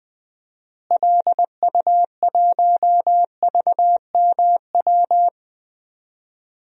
Morse code, heard as a keyed tone, about 21 wpm, 700 Hz. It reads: LU1VMW
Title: DXCC-100 Call Signs - 6 Characters - Repeated 3x - 20wpm